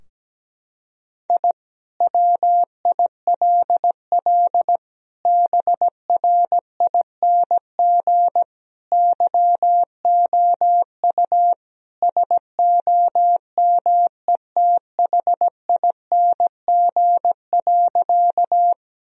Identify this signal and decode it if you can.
Morse code, heard as a keyed tone, about 17 wpm, 700 Hz. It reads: I WILL BRING YOU SOMETHING.
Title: Sentences from Top 1000 Words - Review Part 3 - 17wpm